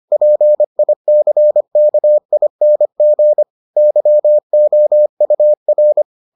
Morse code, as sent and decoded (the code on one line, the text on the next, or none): .--. .. -.-. -.- .. -. --. / -.-- --- ..- .-.
PICKING YOUR